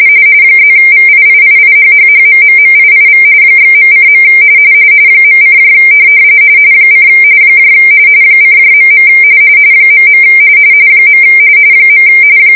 RTTY, about 45.45 baud. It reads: +VIIIIHHHMMMIIV /.:. .: : :.. .. .